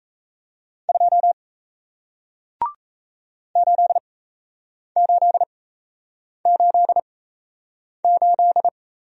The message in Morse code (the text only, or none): 2 8888